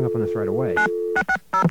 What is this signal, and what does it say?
DTMF, heard as telephone touch tones, tones of about 90 ms, with speech. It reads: 9660